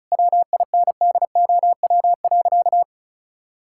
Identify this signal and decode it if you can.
Morse code, heard as a keyed tone, about 36 wpm, 700 Hz. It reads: WINDOW.
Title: Sentences from Top 1600 Words 35wpm